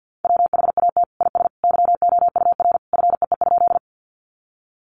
Morse code, heard as a keyed tone, about 50 wpm, 700 Hz. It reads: WHAT IS YOUR FEE?
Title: Sentences from Top 1500 Words 50wpm